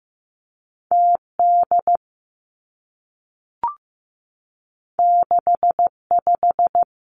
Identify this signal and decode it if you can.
Morse code, heard as a keyed tone, about 15 wpm, 700 Hz. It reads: T D 6 5
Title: Two Challenging Letter-Number Combinations 15wpm